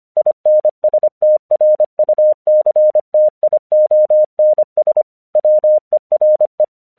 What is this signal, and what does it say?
Morse code, heard as a keyed tone, about 25 wpm, 610 Hz. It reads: INSTRUCTIONS WERE
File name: 1 English 25wpm / AQC26